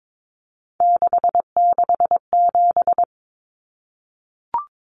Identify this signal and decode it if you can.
Morse code, heard as a keyed tone, about 22 wpm, 700 Hz. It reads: 667
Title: Common Cut Numbers - 3-Digit 22wpm